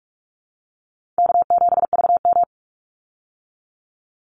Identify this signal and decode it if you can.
Morse code, heard as a keyed tone, about 44 wpm, 700 Hz. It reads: K7VM
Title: Easy Call Signs Repeated 2x - 45wpm